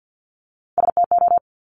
Morse code, heard as a keyed tone, about 50 wpm, 700 Hz. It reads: STO
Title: Top Three Letter Combinations in Words 50wpm